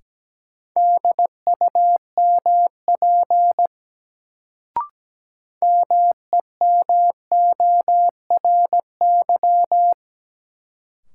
Morse code, heard as a keyed tone, about 17 wpm, 700 Hz. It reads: DUMP MEMORY
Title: Sentences from Top 1300 Words - New Word Drill - 17wpm